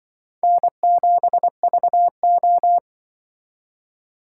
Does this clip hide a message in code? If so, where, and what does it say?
Yes: Morse code from 0.4 to 2.8 s, heard as a keyed tone, about 25 wpm, 700 Hz. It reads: N7VO